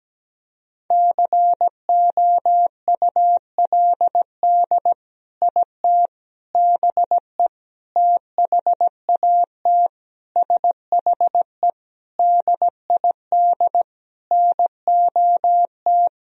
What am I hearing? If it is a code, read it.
Morse code, heard as a keyed tone, about 17 wpm, 700 Hz. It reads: COULD IT BE THAT SHE DID NOT
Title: Sentences from Top 1000 Words - Review Part 1 - 17wpm